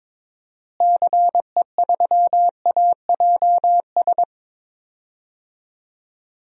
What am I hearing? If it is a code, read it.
Morse code, heard as a keyed tone, about 22 wpm, 700 Hz. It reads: CE3AJS
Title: DXCC-100 Call Signs - 6 Characters - Repeated 3x - 22wpm